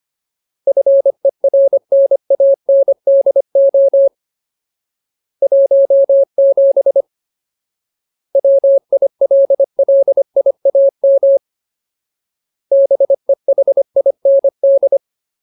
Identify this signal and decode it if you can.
Morse code, heard as a keyed tone, about 25 wpm, 550 Hz. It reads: FERNANDO 17 WILLIAM BEHIND
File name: Ultimate Word List / 25WPM Ultimate